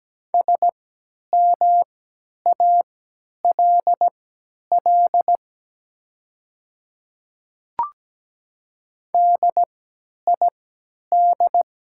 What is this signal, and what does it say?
Morse code, heard as a keyed tone, about 17 wpm, spaced as for 10 wpm, 690 Hz. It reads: SMALL DID
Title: 3x Character Spacing - Top 500 Words - 17wpm